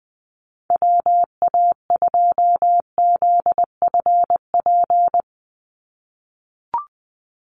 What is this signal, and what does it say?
Morse code, heard as a keyed tone, about 20 wpm, 700 Hz. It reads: WA2ZFP
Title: USA Call Signs 20wpm